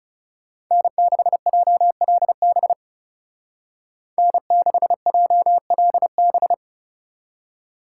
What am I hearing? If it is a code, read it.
Morse code, heard as a keyed tone, about 36 wpm, 700 Hz. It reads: N6JLB N6JLB
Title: Speed Racing - USA 1x3 Call Signs 30wpm